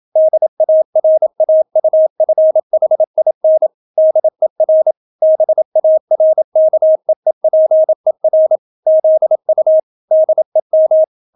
Morse code, heard as a keyed tone, about 27 wpm, 630 Hz. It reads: DARAUFHIN DER BARKEEPER ZU DEM